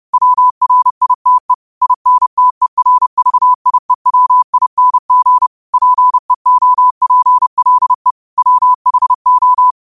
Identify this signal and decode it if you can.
Morse code, heard as a keyed tone, about 30 wpm, 1000 Hz. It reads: WRITE INTERVIEWING PEOPLE WHO